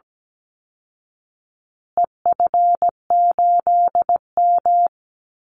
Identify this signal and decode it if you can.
Morse code, heard as a keyed tone, about 17 wpm, 700 Hz. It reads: EF8M
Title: Call Signs Repeated 2x - 17wpm